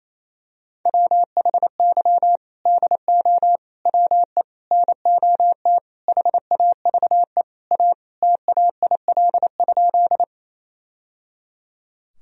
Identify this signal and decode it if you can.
Morse code, heard as a keyed tone, about 28 wpm, 700 Hz. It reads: WHY DO WE NOT HAVE A TAIL?
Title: Sentences from Top 1000 Words - Review Part 3 - 28wpm